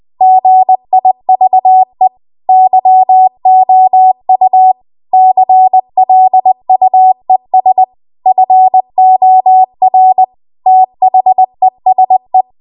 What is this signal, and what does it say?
Morse code, heard as a keyed tone, about 20 wpm, 750 Hz. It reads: GIVE YOU CLUES FOR THESE
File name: Archive / 20 WPM / 20WPM